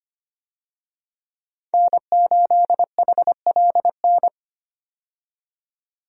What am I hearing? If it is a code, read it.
Morse code, heard as a keyed tone, about 25 wpm, 700 Hz. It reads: N8HLN
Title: USA Call Signs - 5 Characters - 25wpm